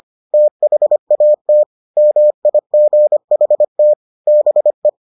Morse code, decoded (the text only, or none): THAT MIGHT BE